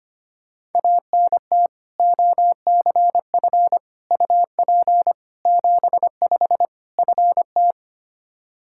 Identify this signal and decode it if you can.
Morse code, heard as a keyed tone, about 25 wpm, 700 Hz. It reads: ANT OCF UP 75 FT
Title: Practice QSO - Exchange 2 - Info - 25wpm